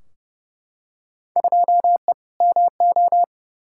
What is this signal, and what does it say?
Morse code, heard as a keyed tone, about 32 wpm, 700 Hz. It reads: 2E MO
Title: Field Day Exchanges - 30wpm